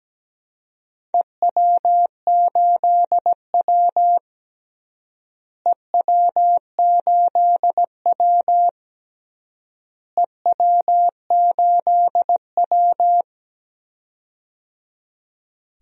Morse code, heard as a keyed tone, about 17 wpm, 700 Hz. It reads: EW8W EW8W EW8W
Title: DXCC-100 Call Signs - 4 Characters - Repeated 3x - 17wpm